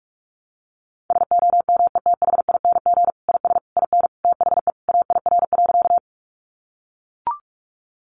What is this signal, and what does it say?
Morse code, heard as a keyed tone, about 44 wpm, 700 Hz. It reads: SOMETHING IS IN THE AIR.
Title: Sentences from Top 600 Words 45wpm